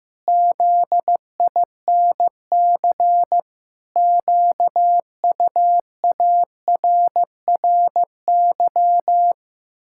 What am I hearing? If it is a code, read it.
Morse code, heard as a keyed tone, about 15 wpm, 700 Hz. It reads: ZINC QUARRY
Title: Pangram Challenge v2 - 15wpm